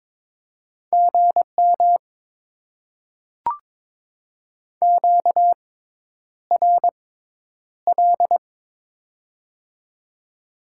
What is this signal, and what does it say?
Morse code, heard as a keyed tone, about 22 wpm, spaced as for 10 wpm, 700 Hz. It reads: GM QRL
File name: Top 90 QSO Elements - 6x Character Spacing - 22wpm